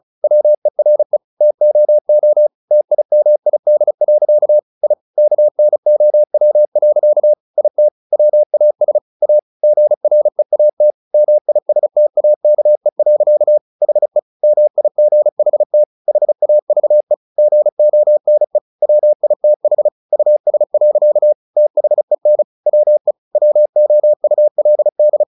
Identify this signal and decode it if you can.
Morse code, heard as a keyed tone, about 34 wpm, 600 Hz. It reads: WERE TOO TIMID. I KNOW. IT WAS A GREAT MISTAKE. HE MIGHT HAVE GONE WITH US. THEN WE WOULD
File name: the old man and the sea / the old man and the sea 1 35wpm